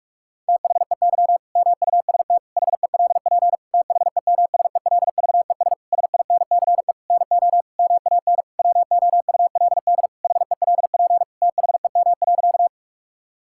Morse code, heard as a keyed tone, about 46 wpm, 700 Hz. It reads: THEY MUST HELP THEMSELVES SINCE NO MAN WOULD HELP THEM.
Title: Sentences from Top 1000 Words - Review Part 5 - 45wpm